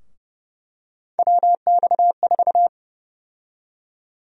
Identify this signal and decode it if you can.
Morse code, heard as a keyed tone, about 30 wpm, 700 Hz. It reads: WX4
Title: USA Call Sign Prefixes - 30wpm